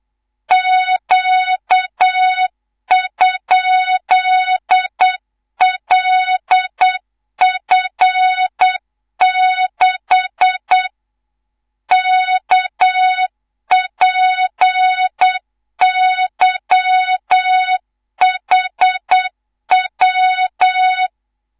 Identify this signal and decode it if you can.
Morse code, heard as a keyed tone, about 8 wpm, 3700 Hz, with noise. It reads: Q?LF6 KPYHW
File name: Morze-HA2MN / hang / 19 GYAKORLAT